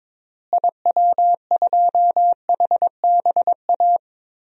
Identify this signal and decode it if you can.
Morse code, heard as a keyed tone, about 22 wpm, 700 Hz. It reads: IW2HBA
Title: DXCC-100 Call Signs - 6 Characters - Repeated 3x - 22wpm